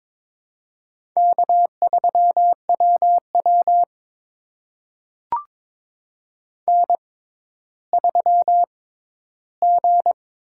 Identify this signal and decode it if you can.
Morse code, heard as a keyed tone, about 22 wpm, spaced as for 12 wpm, 700 Hz. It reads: K3WW N3G
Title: USA Call Signs - 6x Character Spacing - 22wpm